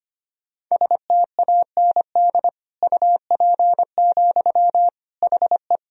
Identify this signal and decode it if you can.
Morse code, heard as a keyed tone, about 25 wpm, 700 Hz. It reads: STAND UP, HE
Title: Sentences from Top 1000 Words - Review Part 1 - 25wpm